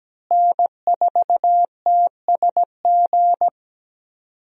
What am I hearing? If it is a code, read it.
Morse code, heard as a keyed tone, about 17 wpm, 700 Hz. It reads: N4TSG
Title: USA Call Signs - 5 Characters - Repeated 3x - 17wpm